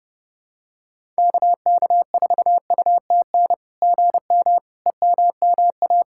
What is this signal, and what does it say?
Morse code, heard as a keyed tone, about 28 wpm, 660 Hz. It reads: KK4UTN GM EMMA
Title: SOTA QSO Practice 30wpm